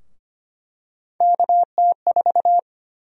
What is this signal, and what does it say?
Morse code, heard as a keyed tone, about 25 wpm, 700 Hz. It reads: KT4
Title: USA Call Sign Prefixes - 25wpm